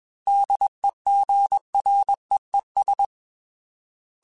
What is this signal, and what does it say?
Morse code, heard as a keyed tone, about 21 wpm, 800 Hz. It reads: DEGREES
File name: Weather Sets / Weather Sets-mp3s / Weather 4 10 @20